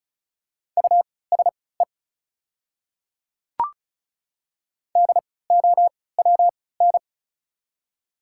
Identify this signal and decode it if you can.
Morse code, heard as a keyed tone, about 36 wpm, spaced as for 20 wpm, 700 Hz. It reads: USE DOWN
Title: 3x Character Spacing - Top 200 Words - 35wpm